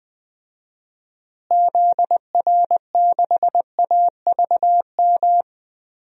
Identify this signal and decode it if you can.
Morse code, heard as a keyed tone, about 20 wpm, 700 Hz. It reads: ZR6AVM
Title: DXCC-100 Call Signs - 6 Characters - Repeated 3x - 20wpm